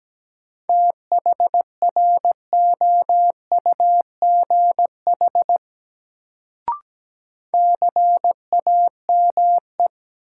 Morse code, heard as a keyed tone, about 17 wpm, 700 Hz. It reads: THROUGH CAME
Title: Sentences from Top 200 Words - New Word Drill - 17wpm